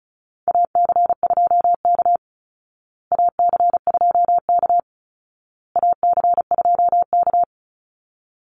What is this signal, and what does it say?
Morse code, heard as a keyed tone, about 34 wpm, 700 Hz. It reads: AC2K AC2K AC2K
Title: Call Signs Repeated 2x - 35wpm